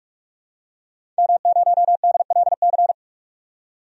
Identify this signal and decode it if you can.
Morse code, heard as a keyed tone, about 46 wpm, 700 Hz. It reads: M0DLC
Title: DXCC-100 Call Signs - 5 Characters - 45wpm